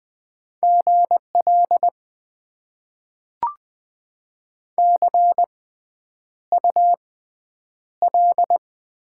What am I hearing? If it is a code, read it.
Morse code, heard as a keyed tone, about 20 wpm, spaced as for 9 wpm, 700 Hz. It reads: GL CUL
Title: Top 90 QSO Elements - 6x Character Spacing - 20wpm